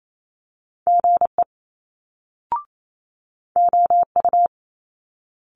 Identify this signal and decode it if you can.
Morse code, heard as a keyed tone, about 28 wpm, 700 Hz. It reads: GE OU
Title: Multi-letter Phonemes Quick Fire 28wpm